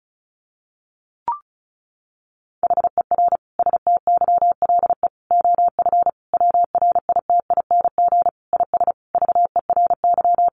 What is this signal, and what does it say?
Morse code, heard as a keyed tone, about 34 wpm, 700 Hz. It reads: HER STYLE OF WRITING IS VERY
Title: Sentences from Top 800 Words 35wpm